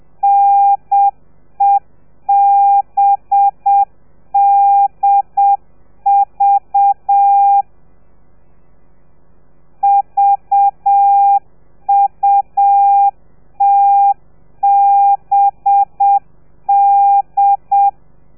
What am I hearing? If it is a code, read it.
Morse code, heard as a keyed tone, about 7 wpm, 800 Hz. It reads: NEBDV VUTBD
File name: telegrafia / i2tfj / Lez4 30